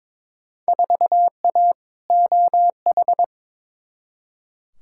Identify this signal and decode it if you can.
Morse code, heard as a keyed tone, about 22 wpm, 700 Hz. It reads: 4A OH
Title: Field Day Exchanges - 22wpm